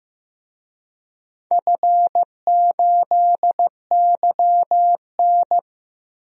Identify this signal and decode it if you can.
Morse code, heard as a keyed tone, about 15 wpm, 700 Hz. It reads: F8YN
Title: DXCC-100 Call Signs - 4 Characters - 15wpm